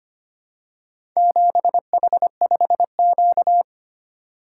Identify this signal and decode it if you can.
Morse code, heard as a keyed tone, about 25 wpm, 700 Hz. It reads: 7H5Q
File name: practice / 25wpm / Four Letter-Number Combinations - 25wpm